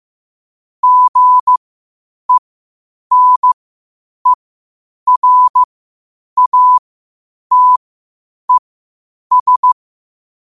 Morse code, heard as a keyed tone, about 15 wpm, spaced as for 8 wpm, 1000 Hz. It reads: GENERATES
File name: qotdfiles / 8wpm